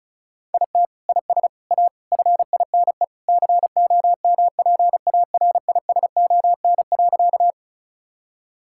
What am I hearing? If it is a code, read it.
Morse code, heard as a keyed tone, about 34 wpm, 700 Hz. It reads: IT IS A FINE COMPARISON.
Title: Sentences from Top 1600 Words 35wpm